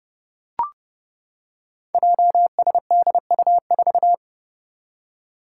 Morse code, heard as a keyed tone, about 30 wpm, 700 Hz. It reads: JSDU4
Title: Five Challenging Letter-Number Combinations 30wpm